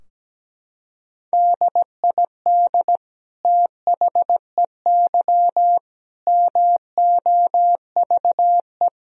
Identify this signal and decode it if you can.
Morse code, heard as a keyed tone, about 17 wpm, 700 Hz. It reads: DID THEY MOVE
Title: Sentences from Top 1000 Words - Review Part 3 - 17wpm